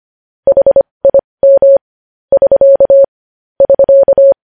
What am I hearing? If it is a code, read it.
Morse code, heard as a keyed tone, about 25 wpm, spaced as for 20 wpm, 550 Hz. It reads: HIM <SK> <SK>
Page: News Headlines in Morse Code at 20 WPM